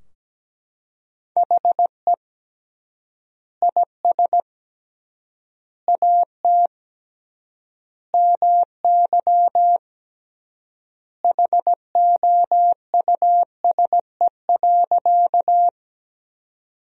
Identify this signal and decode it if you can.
Morse code, heard as a keyed tone, about 17 wpm, 700 Hz. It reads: HE IS AT MY HOUSE.
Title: 3x Word-Spacing - Sentences from Top 200 Words - 17wpm